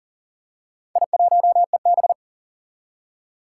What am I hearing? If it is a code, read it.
Morse code, heard as a keyed tone, about 40 wpm, 700 Hz. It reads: I1EB
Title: Four Challenging Letter-Number Combinations 40wpm